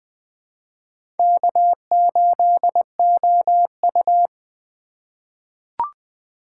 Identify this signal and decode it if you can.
Morse code, heard as a keyed tone, about 20 wpm, 700 Hz. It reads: K8OU
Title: USA Call Signs - 4 Characters - 3x Character Spacing - 20wpm